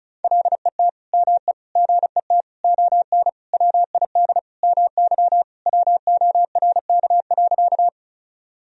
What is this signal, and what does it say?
Morse code, heard as a keyed tone, about 35 wpm, 700 Hz. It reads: LET ME GET ON WID MY WORK.
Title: Sentences from Top 300 Words and Abbreviations 35wpm